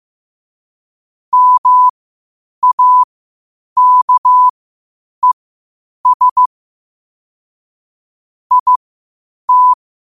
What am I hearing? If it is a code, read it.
Morse code, heard as a keyed tone, about 15 wpm, spaced as for 8 wpm, 1000 Hz. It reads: MAKES IT
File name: qotdfiles / 8wpm